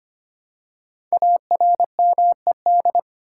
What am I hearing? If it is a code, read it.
Morse code, heard as a keyed tone, about 25 wpm, 700 Hz. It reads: ARMED
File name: 5 Letter Words - First 3 Letters Given - 25wpm